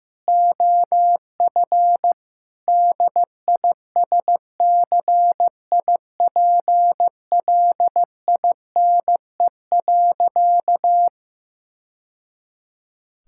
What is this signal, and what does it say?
Morse code, heard as a keyed tone, about 16 wpm, 700 Hz. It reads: OF DISCIPLINE.
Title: Sentences from Top 1000 Words - Review Part 6 - 15wpm